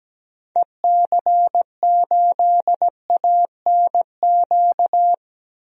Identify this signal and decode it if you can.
Morse code, heard as a keyed tone, about 17 wpm, 700 Hz. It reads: EC8ANQ
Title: DXCC-100 Call Signs - 6 Characters - Repeated 2x - 17wpm